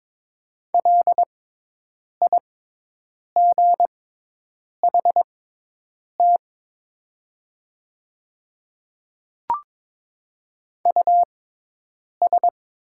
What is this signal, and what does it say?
Morse code, heard as a keyed tone, about 21 wpm, spaced as for 8 wpm, 700 Hz. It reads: LIGHT US
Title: 6x Character Spacing - Top 300 Words - 22wpm